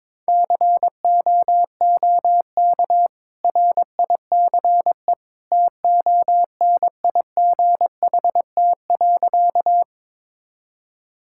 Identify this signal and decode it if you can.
Morse code, heard as a keyed tone, about 22 wpm, 690 Hz. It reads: COOK RICE TONIGHT.
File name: practice / 22wpm / Easy Sentences with Hints - 22wpm